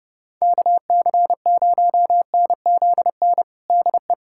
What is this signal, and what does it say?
Morse code, heard as a keyed tone, about 30 wpm, 700 Hz. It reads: KC0NZN DE
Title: USA Callsign DE Callsign 30wpm